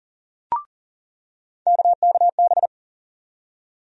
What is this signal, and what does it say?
Morse code, heard as a keyed tone, about 40 wpm, 700 Hz. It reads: KKB
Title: USA Call Sign Postfixes - 3 Letters 40wpm